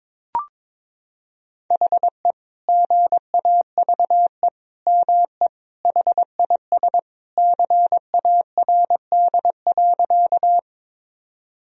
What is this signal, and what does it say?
Morse code, heard as a keyed tone, about 22 wpm, 700 Hz. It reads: HE GAVE ME HIS CARD.
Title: Sentences from Top 1000 Words - Review Part 4 - 22wpm